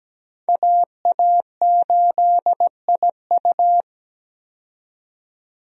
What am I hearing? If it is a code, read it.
Morse code, heard as a keyed tone, about 17 wpm, 700 Hz. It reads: AA8IU
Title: USA Call Signs - 5 Characters - 17wpm